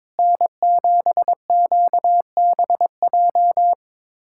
Morse code, heard as a keyed tone, about 22 wpm, 700 Hz. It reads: N7QBJ
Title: USA Callsign DE Callsign - 5 Characters - 22wpm